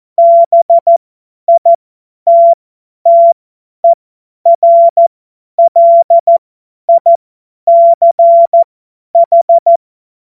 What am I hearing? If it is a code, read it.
Morse code, heard as a keyed tone, about 14 wpm, spaced as for 10 wpm, 680 Hz. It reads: BITTERLICH
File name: Morsekurs / Witze / Witz 08. Februar 2022 50 F70